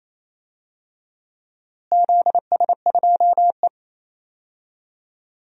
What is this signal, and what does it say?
Morse code, heard as a keyed tone, about 28 wpm, 700 Hz. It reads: ZS2E
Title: DXCC-100 Call Signs - 4 Characters - Repeated 3x - 28wpm